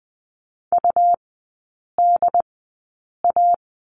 Morse code, heard as a keyed tone, about 20 wpm, spaced as for 8 wpm, 700 Hz. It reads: UDA
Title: 172 Lesson 28e - Number 9 - Three Characters 20wpm